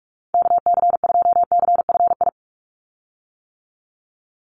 Morse code, heard as a keyed tone, about 46 wpm, 700 Hz. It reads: KC2CFI
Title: USA Call Signs 45wpm